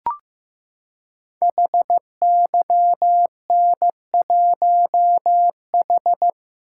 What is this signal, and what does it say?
Morse code, heard as a keyed tone, about 15 wpm, 700 Hz. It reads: HYN1H
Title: Code Groups - 5 Letter:Numbers - 15wpm